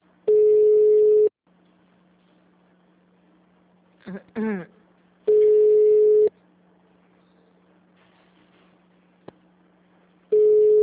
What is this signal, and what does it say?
Ringback tone (Europe, 425 Hz).